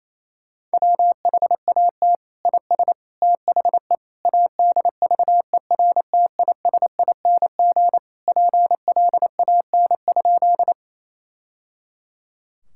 Morse code, heard as a keyed tone, about 28 wpm, 700 Hz. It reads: WHAT IS THE ADVERTISING PLAN?
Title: Sentences from Top 1000 Words - Review Part 6 - 28wpm